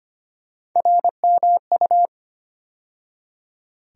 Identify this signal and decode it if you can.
Morse code, heard as a keyed tone, about 25 wpm, 700 Hz. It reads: RMU
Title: USA Call Sign Postfixes - 3 Letters 25wpm